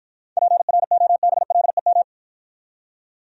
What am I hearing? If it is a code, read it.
Morse code, heard as a keyed tone, about 53 wpm, 700 Hz. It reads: PROBLEM